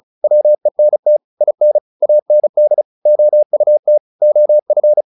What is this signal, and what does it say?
Morse code, heard as a keyed tone, about 34 wpm, 600 Hz. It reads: WENT IN AND OUT OF
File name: the old man and the sea 2 35wpm